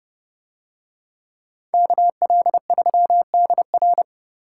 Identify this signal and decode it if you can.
Morse code, heard as a keyed tone, about 28 wpm, 700 Hz. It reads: KL3DR